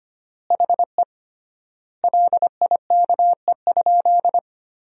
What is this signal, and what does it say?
Morse code, heard as a keyed tone, about 25 wpm, 700 Hz. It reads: HE LIKE?
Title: 3x Word-Spacing - Sentences from Top 100 Words - 25wpm